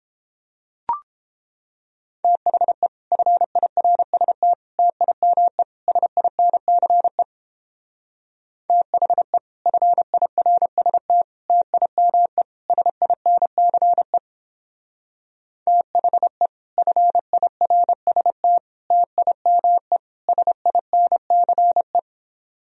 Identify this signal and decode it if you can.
Morse code, heard as a keyed tone, about 31 wpm, 700 Hz. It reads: THE FIRST TIME SINCE THE FIRST TIME SINCE THE FIRST TIME SINCE